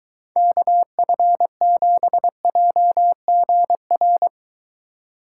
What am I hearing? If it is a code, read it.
Morse code, heard as a keyed tone, about 24 wpm, 700 Hz. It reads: KF7JGR